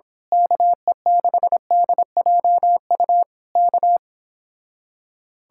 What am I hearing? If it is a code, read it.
Morse code, heard as a keyed tone, about 26 wpm, 700 Hz. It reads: KE6DJU K
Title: Speed-Racing Common QSO Phrases Encore - 22wpm